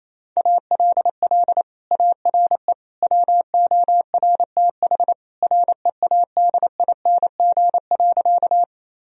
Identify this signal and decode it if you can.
Morse code, heard as a keyed tone, about 27 wpm, 700 Hz. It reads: ALL ARE WORTH READING.
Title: Sentences from Top 1000 Words - Review Part 3 - 28wpm